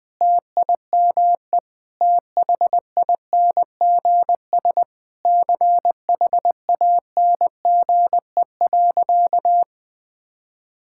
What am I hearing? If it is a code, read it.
Morse code, heard as a keyed tone, about 20 wpm, 700 Hz. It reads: TIME THINGS CHANGE.
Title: Sentences with 6-Word Phrases - 20wpm